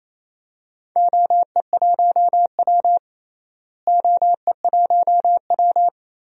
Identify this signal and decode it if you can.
Morse code, heard as a keyed tone, about 28 wpm, 700 Hz. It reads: OE1W OE1W
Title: DXCC-100 Call Signs - 4 Characters - Repeated 2x - 28wpm